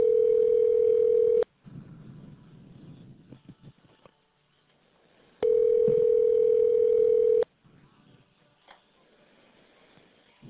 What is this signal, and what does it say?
Ringback tone (North America), 440 + 480 Hz.